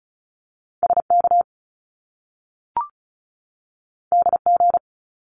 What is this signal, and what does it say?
Morse code, heard as a keyed tone, about 36 wpm, 700 Hz. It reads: SK DG
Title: Top Two Letter Combinations in Words 35wpm